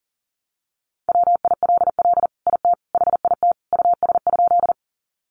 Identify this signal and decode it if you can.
Morse code, heard as a keyed tone, about 40 wpm, 700 Hz. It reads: WILL IT HIT US?
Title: Sentences from Top 1200 Words 40wpm